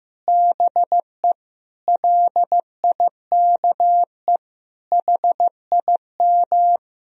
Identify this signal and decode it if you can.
Morse code, heard as a keyed tone, about 15 wpm, 700 Hz. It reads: BE LIKE HIM
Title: Sentences from Top 1000 Words - Review Part 1 - 15wpm